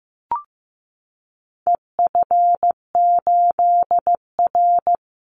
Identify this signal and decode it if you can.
Morse code, heard as a keyed tone, about 15 wpm, 700 Hz. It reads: EF8R
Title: Call Signs Repeated 2x - 15wpm